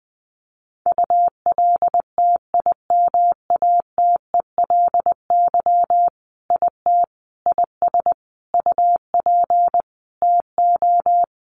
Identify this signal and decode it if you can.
Morse code, heard as a keyed tone, about 19 wpm, 700 Hz. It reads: ULTIMATELY IT IS UP TO
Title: Sentences from Top 1200 Words 20wpm